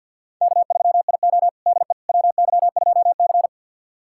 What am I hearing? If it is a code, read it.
Morse code, heard as a keyed tone, about 46 wpm, 700 Hz. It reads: K3IO DE WY2C
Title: USA Callsign DE Callsign - 4 Characters - 50wpm